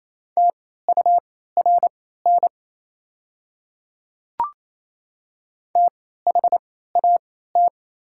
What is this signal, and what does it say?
Morse code, heard as a keyed tone, about 28 wpm, spaced as for 16 wpm, 690 Hz. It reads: TURN THAT
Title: 3x Character Spacing - Top 500 Words - 28wpm